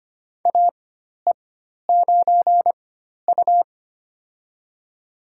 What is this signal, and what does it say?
Morse code, heard as a keyed tone, about 25 wpm, spaced as for 12 wpm, 700 Hz. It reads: AE9U
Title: USA Call Signs - 4 Characters - 4x Character Spacing - 25wpm